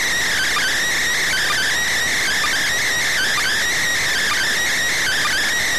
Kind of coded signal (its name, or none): Emergency Alert System (SAME) tones